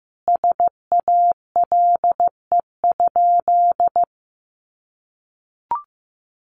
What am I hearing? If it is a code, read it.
Morse code, heard as a keyed tone, about 14 wpm, 700 Hz. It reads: SALE?
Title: Sentences from Top 1400 Words 15wpm